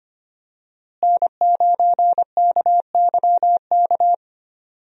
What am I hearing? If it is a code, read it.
Morse code, heard as a keyed tone, about 25 wpm, 700 Hz. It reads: N9KYK
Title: USA Call Signs - 5 Characters - Repeated 2x - 25wpm